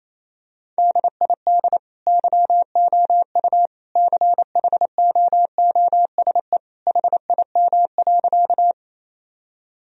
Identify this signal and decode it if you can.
Morse code, heard as a keyed tone, about 28 wpm, 700 Hz. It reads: DID YOU CHOOSE HIM.
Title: Sentences from Top 1000 Words - Review Part 4 - 28wpm